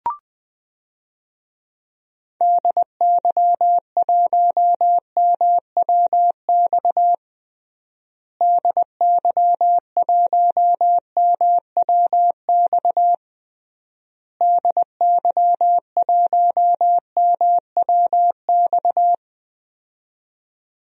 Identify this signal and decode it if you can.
Morse code, heard as a keyed tone, about 20 wpm, 700 Hz. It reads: DY1MWX DY1MWX DY1MWX
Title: DXCC-100 Call Signs - 6 Characters - Repeated 3x - 20wpm